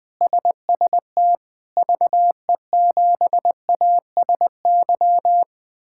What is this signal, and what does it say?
Morse code, heard as a keyed tone, about 21 wpm, 700 Hz. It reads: SST VE7ASY
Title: K1USN SST QSO Practice - 20wpm